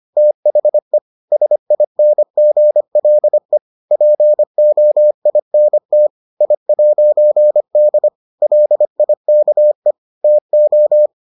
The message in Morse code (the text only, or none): THE SINGLE POINT I'D LIKE TO